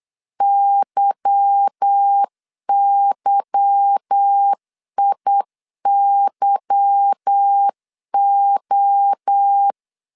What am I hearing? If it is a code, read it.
Morse code, heard as a keyed tone, about 8 wpm, 800 Hz. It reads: YYIYO
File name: LEZIONI / 8 YJ45 / 8 Y 8wpm 3245